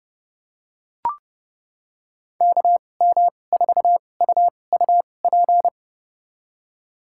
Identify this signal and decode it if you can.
Morse code, heard as a keyed tone, about 31 wpm, spaced as for 22 wpm, 700 Hz. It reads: KM4UUP
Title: USA Call Signs - 6 Characters - 2x Character Spacing - 30wpm